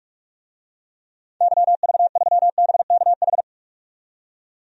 Morse code, heard as a keyed tone, about 46 wpm, 700 Hz. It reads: YV3BKH
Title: DXCC-100 Call Signs - 6 Characters - 45wpm